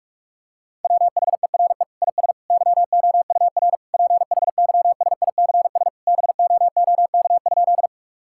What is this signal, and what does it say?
Morse code, heard as a keyed tone, about 41 wpm, 700 Hz. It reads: WHERE IS YOUR PHYSICS BOOK?
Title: Sentences from Top 1000 Words - Review Part 5 - 45wpm